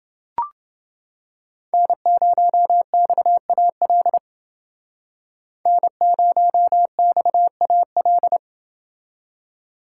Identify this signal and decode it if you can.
Morse code, heard as a keyed tone, about 29 wpm, 700 Hz. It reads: N0XAL N0XAL